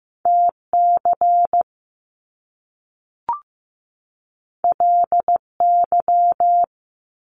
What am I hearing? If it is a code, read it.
Morse code, heard as a keyed tone, about 15 wpm, 700 Hz. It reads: TC LY
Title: Top Two Letter Combinations in Words 15wpm